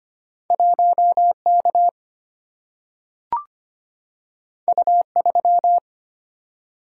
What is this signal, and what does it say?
Morse code, heard as a keyed tone, about 25 wpm, 700 Hz. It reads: 1K U3
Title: Single-Letter Single-Number 25wpm